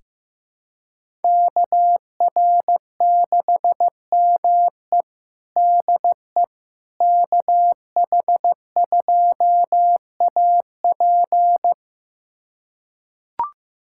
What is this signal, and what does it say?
Morse code, heard as a keyed tone, about 15 wpm, 700 Hz. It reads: KR6ME DE KH2AP
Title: USA Callsign DE Callsign - 5 Characters - 15wpm